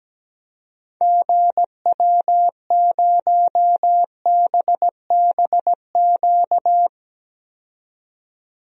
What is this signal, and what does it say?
Morse code, heard as a keyed tone, about 17 wpm, 700 Hz. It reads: GW0BBQ
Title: DXCC-100 Call Signs - 6 Characters - 17wpm